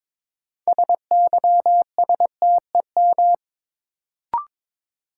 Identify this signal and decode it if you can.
Morse code, heard as a keyed tone, about 22 wpm, 700 Hz. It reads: SYSTEM
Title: Sentences from Top 1000 Words - Review Part 3 - New Word Drill - 22wpm